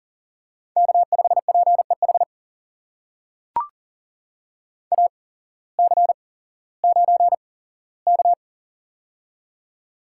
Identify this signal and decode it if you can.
Morse code, heard as a keyed tone, about 40 wpm, spaced as for 19 wpm, 700 Hz. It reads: K5PEH AC9K
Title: USA Call Signs - 8x Character Spacing - 40wpm